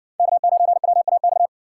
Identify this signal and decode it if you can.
Morse code, heard as a keyed tone, about 63 wpm, 700 Hz. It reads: K9WAX